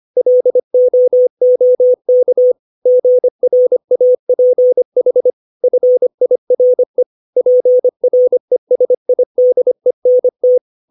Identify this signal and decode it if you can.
Morse code, heard as a keyed tone, about 25 wpm, 500 Hz. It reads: LOOK GRAPH FIRE PRESIDENT
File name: 632 words / 632 Words 25WPM